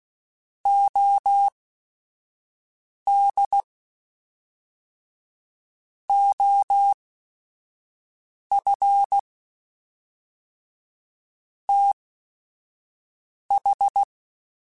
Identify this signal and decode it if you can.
Morse code, heard as a keyed tone, about 16 wpm, spaced as for 5 wpm, 800 Hz. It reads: OD OF TH
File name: Technical Subjects / Technical Subjects-mp3s / AND Gate @5